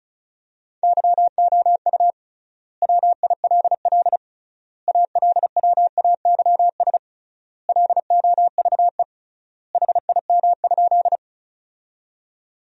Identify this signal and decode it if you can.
Morse code, heard as a keyed tone, about 36 wpm, 700 Hz. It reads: YOU WILL ALWAYS LOVE HIM?
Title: 3x Word-Spacing - Sentences from Top 400 Words - 35wpm